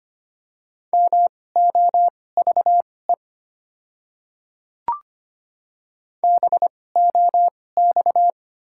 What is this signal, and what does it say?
Morse code, heard as a keyed tone, about 25 wpm, spaced as for 18 wpm, 690 Hz. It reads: MOVE BOX
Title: 2x Character Spacing - Top 500 Words - 25wpm